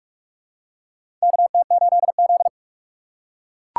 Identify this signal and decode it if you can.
Morse code, heard as a keyed tone, about 42 wpm, 700 Hz. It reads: KT8Z
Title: USA Call Signs - 4 Characters - 2x Character Spacing - 45wpm